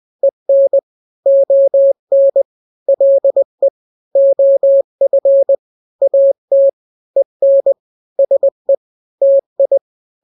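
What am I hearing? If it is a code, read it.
Morse code, heard as a keyed tone, about 20 wpm, 550 Hz. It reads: EN ON LE OF AT EN SE TI
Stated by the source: Morse drill of two-letter n-grams at 20/18 WPM (Farnsworth), from KK7RR.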